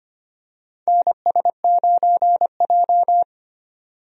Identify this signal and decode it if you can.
Morse code, heard as a keyed tone, about 25 wpm, 700 Hz. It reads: NS9J
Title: USA Call Signs - 4 Characters - 25wpm